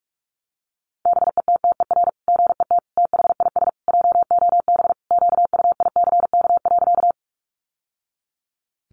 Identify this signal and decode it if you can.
Morse code, heard as a keyed tone, about 44 wpm, 700 Hz. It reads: BETTER GET THIS JOB QUICK.
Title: Sentences from Top 600 Words 45wpm